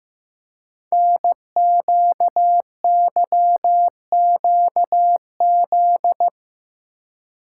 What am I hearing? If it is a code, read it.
Morse code, heard as a keyed tone, about 15 wpm, 700 Hz. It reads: NQYQZ
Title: Code Groups - 5 Letters - 15wpm